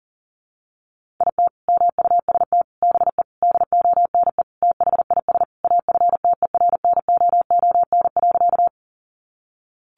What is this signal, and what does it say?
Morse code, heard as a keyed tone, about 40 wpm, 700 Hz. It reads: IT MUST BE DONE THIS AFTERNOON.
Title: Sentences from Top 1500 Words 40wpm